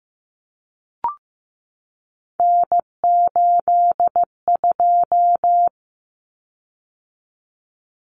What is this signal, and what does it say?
Morse code, heard as a keyed tone, about 15 wpm, 700 Hz. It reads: N82
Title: Common Cut Numbers - 3-Digit 15wpm